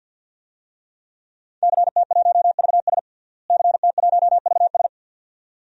Morse code, heard as a keyed tone, about 51 wpm, 700 Hz. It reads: CT1VS CT1VS
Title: DXCC-100 Call Signs - 5 Characters - Repeated 2x - 50wpm